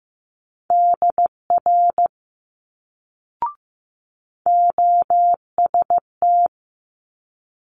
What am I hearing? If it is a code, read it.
Morse code, heard as a keyed tone, about 15 wpm, 700 Hz. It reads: DR OST
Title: Multi-letter Phonemes Quick Fire 15wpm